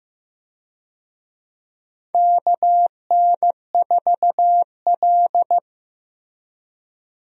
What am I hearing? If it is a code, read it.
Morse code, heard as a keyed tone, about 15 wpm, 700 Hz. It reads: KN4L
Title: USA Call Signs - 4 Characters - 15wpm